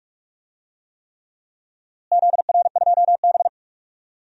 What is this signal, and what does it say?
Morse code, heard as a keyed tone, about 42 wpm, 690 Hz. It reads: ZR2B
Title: DXCC-100 Call Signs - 4 Characters - Repeated 3x - 45wpm